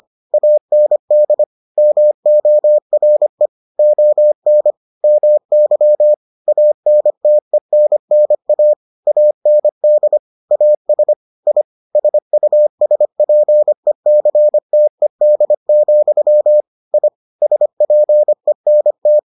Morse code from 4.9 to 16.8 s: MY ANTENNA AND AS I SUSPECTED,

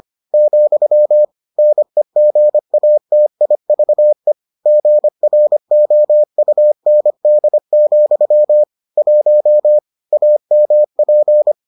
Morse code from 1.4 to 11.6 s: NEGATIVE GROUND, 1 AMP

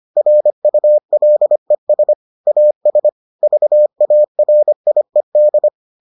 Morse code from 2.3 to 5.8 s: AS VARIED

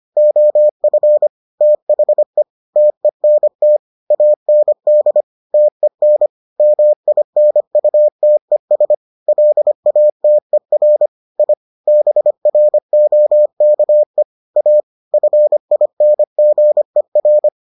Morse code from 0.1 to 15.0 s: OF THE TENT AND TEN MINUTES LATER I BROKE A

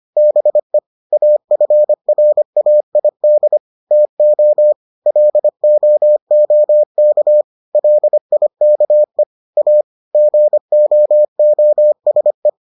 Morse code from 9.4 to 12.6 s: A GOOSE

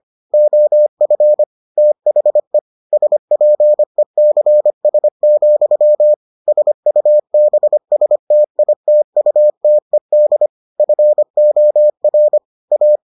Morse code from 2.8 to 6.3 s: SPECS,

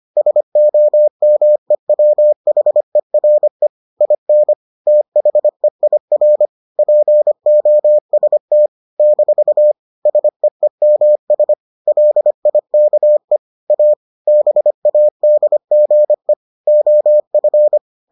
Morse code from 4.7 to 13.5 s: THEIR POST - SEEMS LIKE

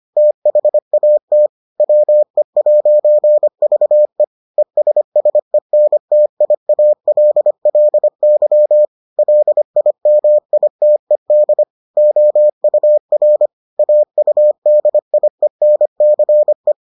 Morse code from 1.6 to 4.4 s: WE'VE

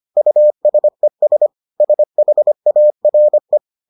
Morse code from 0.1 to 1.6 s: USES

